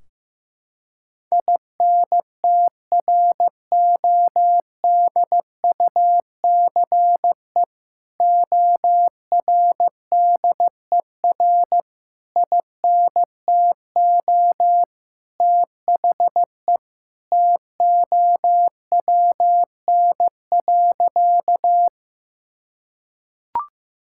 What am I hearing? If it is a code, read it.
Morse code, heard as a keyed tone, about 16 wpm, 700 Hz. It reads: INTRODUCE ORDER INTO THE TOWN.
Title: Sentences from Top 1600 Words 15wpm